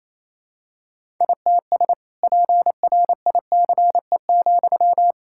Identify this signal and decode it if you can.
Morse code, heard as a keyed tone, about 28 wpm, 700 Hz. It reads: ITS PRICE,
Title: Sentences from Top 1000 Words - Review Part 5 - 28wpm